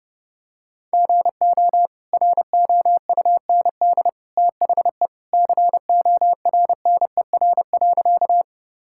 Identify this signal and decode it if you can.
Morse code, heard as a keyed tone, about 30 wpm, 700 Hz. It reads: GO ROUND THE CORNER.
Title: Sentences from Top 1600 Words 30wpm